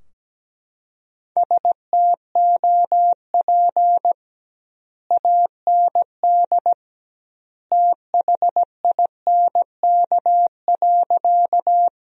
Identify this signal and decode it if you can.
Morse code, heard as a keyed tone, about 17 wpm, 700 Hz. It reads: STOP AND THINK.
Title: 2x Word-Spacing - Sentences from Top 300 Words - 17wpm